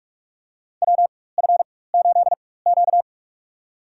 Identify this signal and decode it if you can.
Morse code, heard as a keyed tone, about 47 wpm, spaced as for 21 wpm, 700 Hz. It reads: WF8Q
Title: USA Call Signs - 4 Characters - 4x Character Spacing - 45wpm